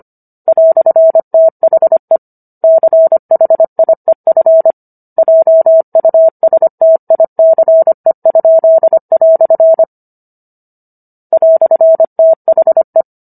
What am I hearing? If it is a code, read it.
Morse code, heard as a keyed tone, about 25 wpm, 650 Hz. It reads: "THE CHIEF JUSTICE?" "THE